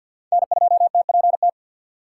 Morse code, heard as a keyed tone, about 47 wpm, 700 Hz. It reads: N2TPT
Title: USA Call Signs - 5 Characters - 2x Character Spacing - 50wpm